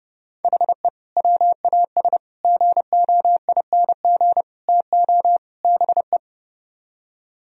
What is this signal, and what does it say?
Morse code, heard as a keyed tone, about 30 wpm, 700 Hz. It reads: HE WAS GOING TO BE